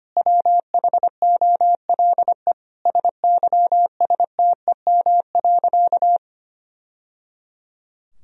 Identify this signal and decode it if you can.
Morse code, heard as a keyed tone, about 25 wpm, 700 Hz. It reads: WHOLE SYSTEM.
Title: Sentences from Top 1000 Words - Review Part 3 - 25wpm